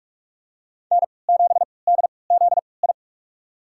Morse code, heard as a keyed tone, about 44 wpm, spaced as for 25 wpm, 700 Hz. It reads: N7DZI
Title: USA Call Signs - 5 Characters - 3x Character Spacing - 45wpm